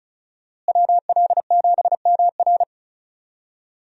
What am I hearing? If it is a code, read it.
Morse code, heard as a keyed tone, about 34 wpm, 700 Hz. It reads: WL7MR
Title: USA Call Signs - 5 Characters - Repeated 3x - 35wpm